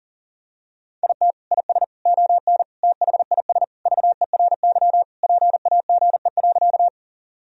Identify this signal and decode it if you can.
Morse code, heard as a keyed tone, about 40 wpm, 700 Hz. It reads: IT IS ON THIS VERY PAGE.
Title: Sentences from Top 1000 Words - Review Part 1 - 40wpm